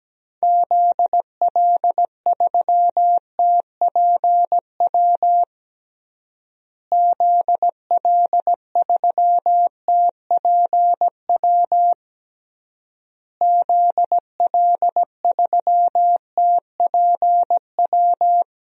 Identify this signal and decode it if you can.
Morse code, heard as a keyed tone, about 17 wpm, 700 Hz. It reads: ZL3TPW ZL3TPW ZL3TPW
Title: DXCC-100 Call Signs - 6 Characters - Repeated 3x - 17wpm